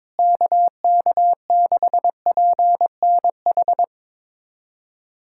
Morse code, heard as a keyed tone, about 22 wpm, 700 Hz. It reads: KK6PNH
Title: USA Callsign DE Callsign - 6 Characters - 22wpm